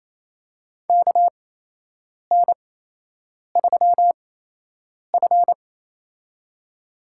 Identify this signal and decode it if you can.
Morse code, heard as a keyed tone, about 28 wpm, spaced as for 8 wpm, 700 Hz. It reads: KN3F
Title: USA Call Signs - 4 Characters - 8x Character Spacing - 28wpm